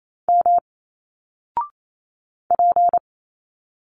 Morse code, heard as a keyed tone, about 28 wpm, 700 Hz. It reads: M P
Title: Single Letters 28wpm